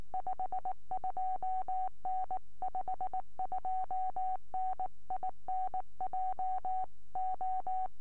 Morse code, heard as a keyed tone, about 19 wpm, 750 Hz. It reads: 52N 52N IN JO
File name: ok1mac 4m